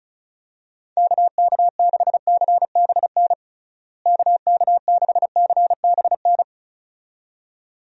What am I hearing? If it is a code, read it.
Morse code, heard as a keyed tone, about 36 wpm, 700 Hz. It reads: KK6CBN KK6CBN
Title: USA Call Signs - 6 Characters - Repeated 2x - 35wpm